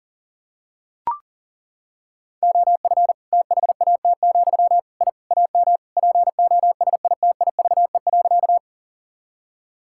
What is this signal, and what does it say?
Morse code, heard as a keyed tone, about 37 wpm, 700 Hz. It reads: OF THAT, I AM POSITIVE.
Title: Sentences from Top 1000 Words - Review Part 5 - 40wpm